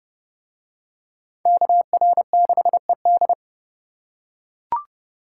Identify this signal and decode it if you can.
Morse code, heard as a keyed tone, about 29 wpm, 700 Hz. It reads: KR6ED